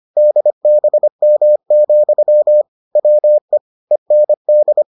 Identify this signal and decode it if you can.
Morse code, heard as a keyed tone, about 25 wpm, 600 Hz. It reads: DBM, WE END